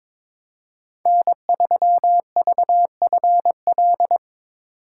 Morse code, heard as a keyed tone, about 22 wpm, 700 Hz. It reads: N3VFL
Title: USA Call Signs - 5 Characters - 3x Character Spacing - 22wpm